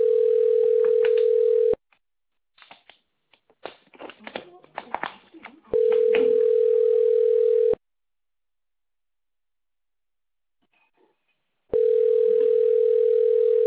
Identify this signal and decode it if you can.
Ringback tone (North America), 440 + 480 Hz.